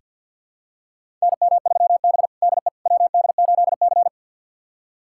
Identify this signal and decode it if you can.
Morse code, heard as a keyed tone, about 47 wpm, 690 Hz. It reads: NM3B DE WD8C
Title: USA Callsign DE Callsign - 4 Characters - 50wpm